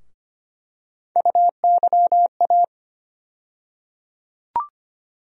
Morse code, heard as a keyed tone, about 25 wpm, 700 Hz. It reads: UYA
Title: USA Call Sign Postfixes - 3 Letters 25wpm